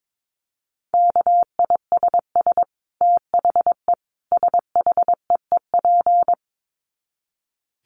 Morse code, heard as a keyed tone, about 22 wpm, 700 Hz. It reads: KISS THE SHEEP
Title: Dit Challenge 22wpm